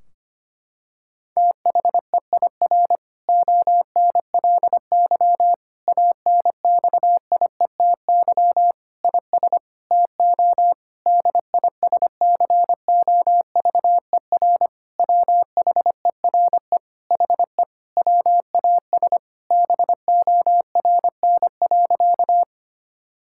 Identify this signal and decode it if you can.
Morse code, heard as a keyed tone, about 25 wpm, 700 Hz. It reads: THEIR ONLY ANXIETY IS TO DISCOVER WHERE HE WAS BORN.
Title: Sentences from Top 1600 Words 25wpm